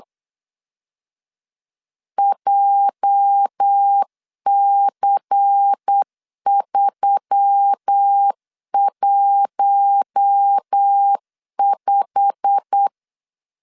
Morse code, heard as a keyed tone, about 8 wpm, 800 Hz. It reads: JC315
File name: CW / DET / 1 9 1500C S B 8wpm 5624